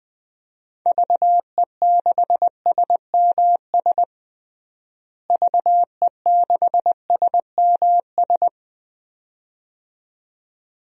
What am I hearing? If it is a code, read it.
Morse code, heard as a keyed tone, about 21 wpm, 700 Hz. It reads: VE6SMS VE6SMS
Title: DXCC-100 Call Signs - 6 Characters - Repeated 2x - 20wpm